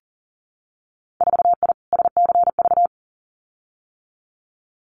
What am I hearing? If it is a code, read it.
Morse code, heard as a keyed tone, about 38 wpm, 700 Hz. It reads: 4I SCV
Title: Winter Field Day Exchanges Encore 40wpm